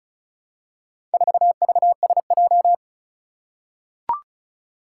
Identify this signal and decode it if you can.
Morse code, heard as a keyed tone, about 36 wpm, 700 Hz. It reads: 4VSJ
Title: Four Challenging Letter-Number Combinations 35wpm